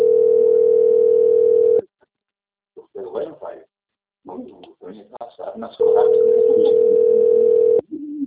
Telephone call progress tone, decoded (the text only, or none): ringback tone